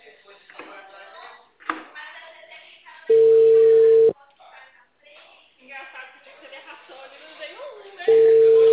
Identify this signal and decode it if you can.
Ringback tone (China, 450 Hz), with speech.
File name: out- -2025- - - .540